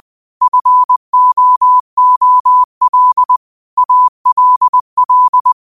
Morse code, heard as a keyed tone, about 20 wpm, 1000 Hz. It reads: FOOL ALL
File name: qotdfiles / 20wpm